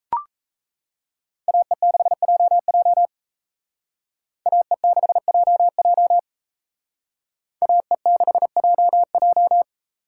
Morse code, heard as a keyed tone, about 34 wpm, 700 Hz. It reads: AE6JJ AE6JJ AE6JJ
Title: Speed Racing - USA 2x2 Call Signs 28wpm